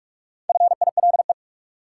Morse code, heard as a keyed tone, about 42 wpm, 700 Hz. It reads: FILE
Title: Sentences from Top 1000 Words - Review Part 4 - New Word Drill - 45wpm